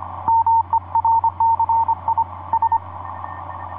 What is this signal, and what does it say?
Morse code, heard as a keyed tone, about 25 wpm, 910 Hz. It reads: MERCI S
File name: oa4-dl1nl pa0gms etc 17m cw 04feb2010 1546